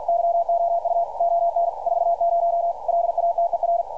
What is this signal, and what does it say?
Morse code, heard as a keyed tone, about 52 wpm, 680 Hz. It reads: YOU ON 40 LATER
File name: archived / CRSnet